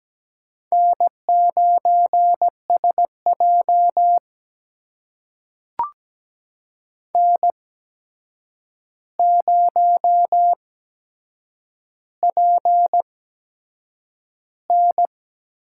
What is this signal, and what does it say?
Morse code, heard as a keyed tone, about 17 wpm, spaced as for 7 wpm, 700 Hz. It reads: N9SJ N0PN
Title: USA Call Signs - 4 Characters - 8x Character Spacing - 17wpm